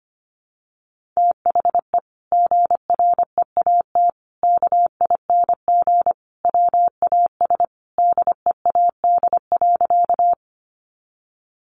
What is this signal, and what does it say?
Morse code, heard as a keyed tone, about 25 wpm, 700 Hz. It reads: THE GREAT KING WAS DEAD.
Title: Sentences from Top 1500 Words 25wpm